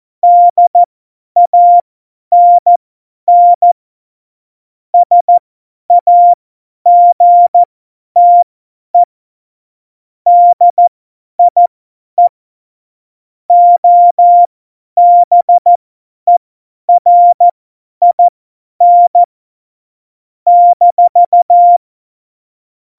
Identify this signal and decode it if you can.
Morse code, heard as a keyed tone, about 14 wpm, spaced as for 10 wpm, 700 Hz. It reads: DANN SAGTE DIE OBERIN -
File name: Morsekurs / Witze / Witz 12. Januar 2021, 50 F70